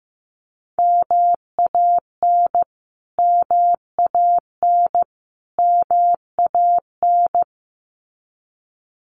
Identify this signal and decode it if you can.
Morse code, heard as a keyed tone, about 15 wpm, 700 Hz. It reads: MAN MAN MAN
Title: Top 500 Words Repeated 15wpm